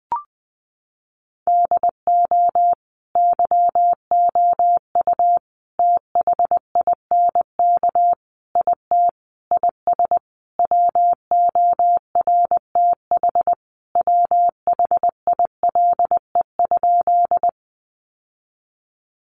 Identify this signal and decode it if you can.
Morse code, heard as a keyed tone, about 20 wpm, 700 Hz. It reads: DO YOU THINK IT IS WORTH WHILE?
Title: Sentences from Top 800 Words 20wpm